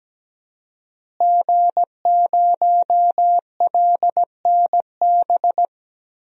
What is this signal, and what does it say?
Morse code, heard as a keyed tone, about 17 wpm, 700 Hz. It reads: G0LNB
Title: DXCC-100 Call Signs - 5 Characters - Repeated 3x - 17wpm